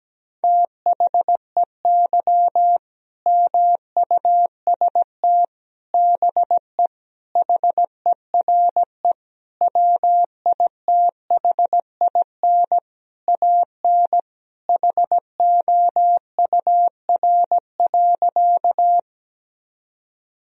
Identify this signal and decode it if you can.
Morse code, heard as a keyed tone, about 17 wpm, 700 Hz. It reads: THEY MUST BE HERE WITHIN AN HOUR.
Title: Sentences from Top 1000 Words - Review Part 4 - 17wpm